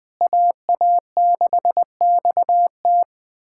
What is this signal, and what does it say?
Morse code, heard as a keyed tone, about 21 wpm, 700 Hz. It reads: AA6XT
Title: USA Callsign DE Callsign 20wpm